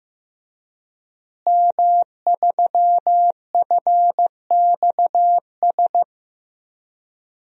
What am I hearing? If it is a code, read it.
Morse code, heard as a keyed tone, about 15 wpm, 700 Hz. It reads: M3FXS
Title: DXCC-100 Call Signs - 5 Characters - Repeated 2x - 15wpm